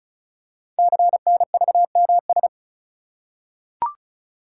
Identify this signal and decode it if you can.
Morse code, heard as a keyed tone, about 36 wpm, 700 Hz. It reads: CNVMS
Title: Code Groups - 5 Letters - 35wpm